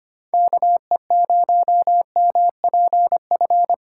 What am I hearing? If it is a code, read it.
Morse code, heard as a keyed tone, about 25 wpm, 700 Hz. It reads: KE0MPF
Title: USA Call Signs - 4x Character Spacing - 25wpm